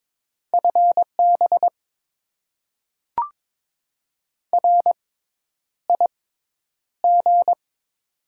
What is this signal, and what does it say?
Morse code, heard as a keyed tone, about 22 wpm, spaced as for 10 wpm, 690 Hz. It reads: FB RIG
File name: Top 20 QSO Elements - 6x Character Spacing - 22wpm